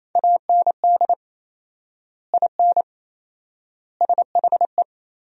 Morse code, heard as a keyed tone, about 28 wpm, 700 Hz. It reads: AND IN SHE
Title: 4x Word-Spacing - Sentences from Top 200 Words - 28wpm